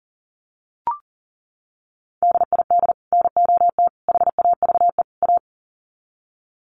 Morse code, heard as a keyed tone, about 40 wpm, 700 Hz. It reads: DID NOT HAVE A
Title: Sets of 4 Words 40wpm